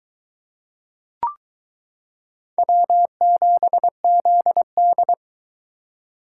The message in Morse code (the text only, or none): W7ZD